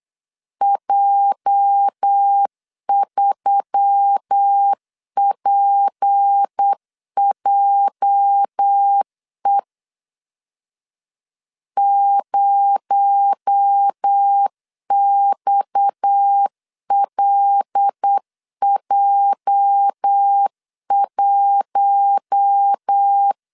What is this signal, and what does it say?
Morse code, heard as a keyed tone, about 8 wpm, 800 Hz. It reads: J3PJE 0XLJ1
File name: LEZIONI / 8 YJ45 / 8 YJ 8wpm 3313